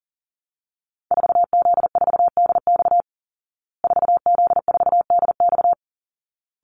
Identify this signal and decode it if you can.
Morse code, heard as a keyed tone, about 40 wpm, 700 Hz. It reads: 4Z4DX 4Z4DX
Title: Call Signs Repeated 3x - 40wpm